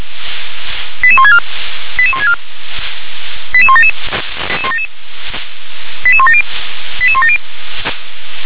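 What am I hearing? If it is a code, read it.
Five-tone selective call (ZVEI), heard as quick runs of five tones, tones of about 70 ms. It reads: 88164 88164 88177 88170 88170
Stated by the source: ZVEI five-tone call received by SDR.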